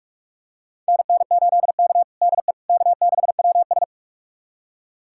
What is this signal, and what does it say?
Morse code, heard as a keyed tone, about 45 wpm, 700 Hz. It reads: NN8K DE K6WS
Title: USA Callsign DE Callsign - 4 Characters - 45wpm